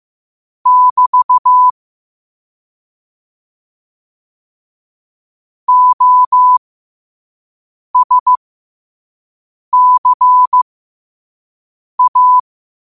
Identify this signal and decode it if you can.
Morse code, heard as a keyed tone, about 15 wpm, spaced as for 5 wpm, 1000 Hz. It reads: <BT> OSCA
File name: qotdfiles / 5wpm